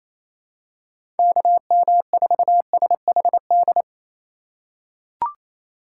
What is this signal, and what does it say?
Morse code, heard as a keyed tone, about 28 wpm, 700 Hz. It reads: KM4SHD
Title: USA Call Signs - 6 Characters - 3x Character Spacing - 28wpm